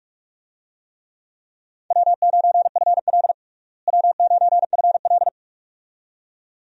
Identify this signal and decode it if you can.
Morse code, heard as a keyed tone, about 46 wpm, 700 Hz. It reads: W9FL W9FL
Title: USA Call Signs - 4 Characters - Repeated 2x - 45wpm